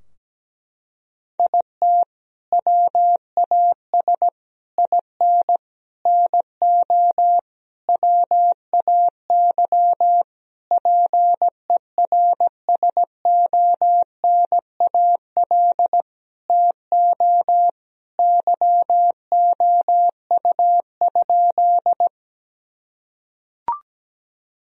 Morse code, heard as a keyed tone, about 17 wpm, 700 Hz. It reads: IT WAS IN NO WAY PERSONAL TO YOU?
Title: Sentences from Top 1000 Words - Review Part 4 - 17wpm